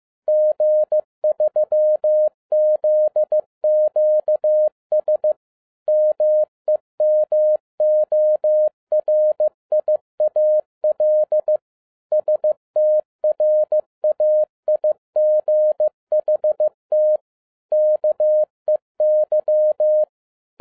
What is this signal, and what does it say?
Morse code, heard as a keyed tone, about 15 wpm, 610 Hz. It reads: G3ZQS MEMORIAL STRAIGHT KEY
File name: 2011-10-mp3 / n12-JL3YMV 15wpm